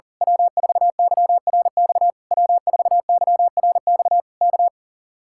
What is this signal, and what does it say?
Morse code, heard as a keyed tone, about 38 wpm, 700 Hz. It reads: W4YRX W4YRX K
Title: Speed-Racing Common QSO Phrases - 40wpm